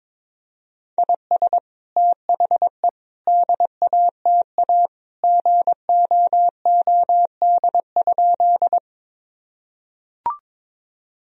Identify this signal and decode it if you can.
Morse code, heard as a keyed tone, about 22 wpm, 700 Hz. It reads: IS THE DATA GOOD?
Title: Sentences from Top 1000 Words - Review Part 3 - 22wpm